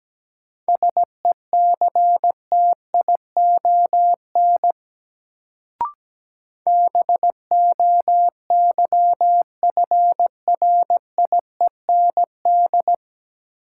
Morse code, heard as a keyed tone, about 17 wpm, 700 Hz. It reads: SECTION BOYFRIEND
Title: Sentences from Top 1000 Words - Review Part 5 - New Word Drill - 17wpm